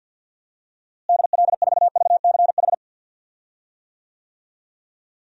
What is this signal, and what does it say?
Morse code, heard as a keyed tone, about 53 wpm, 700 Hz. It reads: DL4VCH
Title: DXCC-100 Call Signs - 6 Characters - Repeated 2x - 50wpm